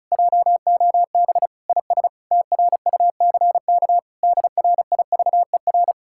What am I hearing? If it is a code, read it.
Morse code, heard as a keyed tone, about 37 wpm, 700 Hz. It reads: JOB IS TRUCK DRIVER
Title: Speed-Racing Common QSO Phrases Encore - 30wpm